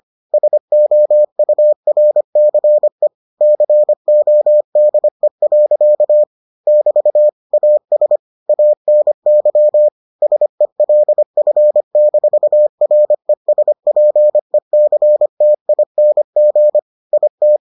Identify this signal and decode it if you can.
Morse code, heard as a keyed tone, about 25 wpm, 600 Hz. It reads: SOURCE CODE. <BT> AS ANY SELF-RESPECTING IT